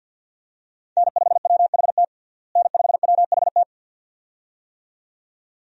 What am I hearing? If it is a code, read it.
Morse code, heard as a keyed tone, about 47 wpm, 700 Hz. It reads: N5WHT N5WHT
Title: USA Call Signs - 5 Characters - Repeated 2x - 50wpm